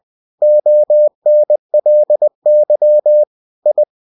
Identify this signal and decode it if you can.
Morse code, heard as a keyed tone, about 20 wpm, 600 Hz. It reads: ONLY I
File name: cw-books / winnie-the-pooh / winnie-20wpm-page-0003